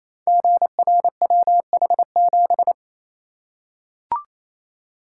Morse code, heard as a keyed tone, about 27 wpm, 700 Hz. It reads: GRWH7
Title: Code Groups - 5 Letter:Numbers - 28wpm